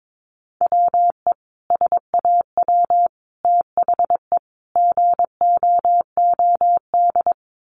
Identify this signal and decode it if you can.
Morse code, heard as a keyed tone, about 22 wpm, 700 Hz. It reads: WE SAW THE GOOD
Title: Sentences from Top 1100 Words 22wpm